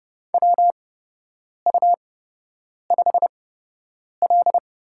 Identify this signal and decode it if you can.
Morse code, heard as a keyed tone, about 30 wpm, spaced as for 8 wpm, 700 Hz. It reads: WU5L
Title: USA Call Signs - 8x Character Spacing - 30wpm